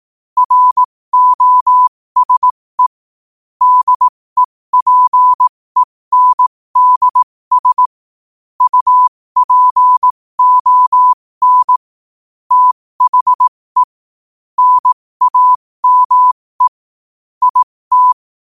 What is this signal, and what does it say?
Morse code, heard as a keyed tone, about 18 wpm, spaced as for 15 wpm, 1000 Hz. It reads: ROSE DEPENDS UPON THE NAME IT
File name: qotdfiles / 15wpm